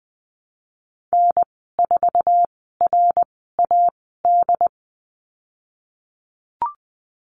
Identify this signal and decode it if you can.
Morse code, heard as a keyed tone, about 20 wpm, spaced as for 15 wpm, 700 Hz. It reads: N4RAD
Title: 069 Lesson 12g - Letter d - Call Signs 20wpm